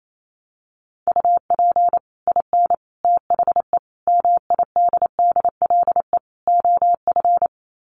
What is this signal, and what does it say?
Morse code, heard as a keyed tone, about 28 wpm, 700 Hz. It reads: UP IN THE MIDDLE OF
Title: Sets of 5 Words Encore 28wpm